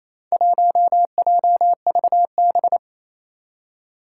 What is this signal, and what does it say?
Morse code, heard as a keyed tone, about 29 wpm, 700 Hz. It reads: 1JVB
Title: Four Challenging Letter-Number Combinations 28wpm